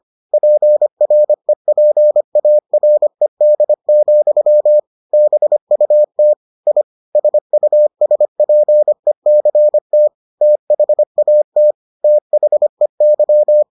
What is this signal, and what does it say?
Morse code, heard as a keyed tone, about 25 wpm, 600 Hz. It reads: PREPARED, BUT I SUSPECT THAT THEY